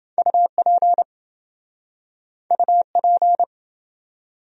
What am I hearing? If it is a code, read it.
Morse code, heard as a keyed tone, about 29 wpm, 700 Hz. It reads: UP UP